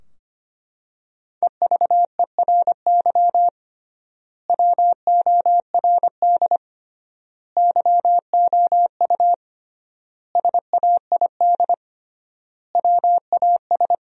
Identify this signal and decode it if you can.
Morse code, heard as a keyed tone, about 25 wpm, 700 Hz. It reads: EVERY WORD YOU SAID WAS
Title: 3x Word-Spacing - Sentences from Top 400 Words - 25wpm